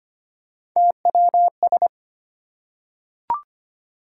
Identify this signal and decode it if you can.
Morse code, heard as a keyed tone, about 25 wpm, 700 Hz. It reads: TWS
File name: Three Letter-Number Combinations - 25wpm